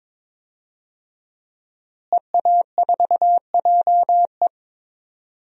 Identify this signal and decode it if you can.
Morse code, heard as a keyed tone, about 21 wpm, 700 Hz. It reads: EA4JE
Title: DXCC-100 Call Signs - 5 Characters - Repeated 2x - 22wpm